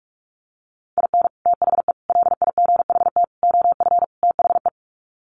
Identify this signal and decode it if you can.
Morse code, heard as a keyed tone, about 44 wpm, 700 Hz. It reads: IN THE LIGHT OF THE
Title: Sets of 5 Words Encore2 45wpm